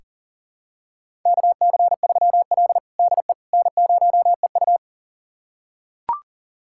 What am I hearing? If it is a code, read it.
Morse code, heard as a keyed tone, about 40 wpm, 700 Hz. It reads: KC3L DE N0EU
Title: USA Callsign DE Callsign - 4 Characters - 40wpm